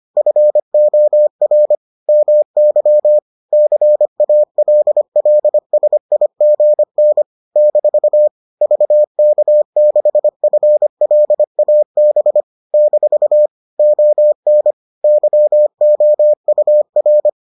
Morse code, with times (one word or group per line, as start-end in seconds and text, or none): 0.2-1.8 FOR
2.1-3.2 MY
3.5-7.2 CALLSIGN
7.6-8.3 -
8.6-12.4 VK6FLAB
12.7-13.5 -
13.8-14.7 ON
15.0-17.3 YOUR